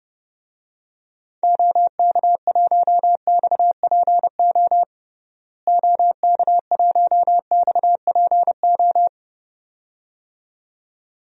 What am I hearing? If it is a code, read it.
Morse code, heard as a keyed tone, about 32 wpm, 700 Hz. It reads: OK1XPO OK1XPO
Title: DXCC-100 Call Signs - 6 Characters - Repeated 2x - 30wpm